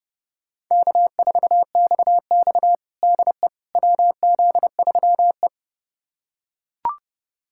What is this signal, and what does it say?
Morse code, heard as a keyed tone, about 30 wpm, 700 Hz. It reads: K4XX DE WZ3E
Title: USA Callsign DE Callsign - 4 Characters - 30wpm